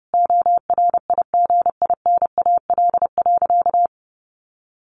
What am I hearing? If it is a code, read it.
Morse code, heard as a keyed tone, about 30 wpm, 700 Hz. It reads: ORIGINAL.
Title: Sentences from Top 1500 Words 30wpm